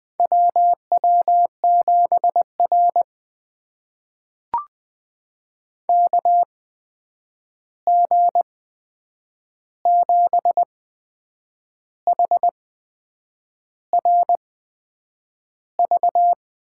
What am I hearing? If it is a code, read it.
Morse code, heard as a keyed tone, about 20 wpm, spaced as for 8 wpm, 700 Hz. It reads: WW7R KG7HRV
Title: USA Call Signs - 8x Character Spacing - 20wpm